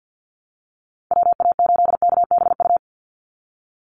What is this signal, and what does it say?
Morse code, heard as a keyed tone, about 50 wpm, 700 Hz. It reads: WA8KBU
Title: Call Signs Repeated 2x - 50wpm